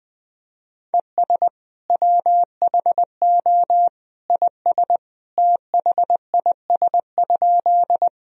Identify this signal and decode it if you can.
Morse code, heard as a keyed tone, about 20 wpm, 700 Hz. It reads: ES WHO IS THIS?
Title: Sentences from Top 100 Words and Abbreviations 20wpm